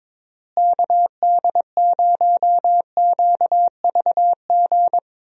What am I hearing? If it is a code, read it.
Morse code, heard as a keyed tone, about 22 wpm, 700 Hz. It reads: KD0QVG
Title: USA Call Signs - 6 Characters - Repeated 3x - 22wpm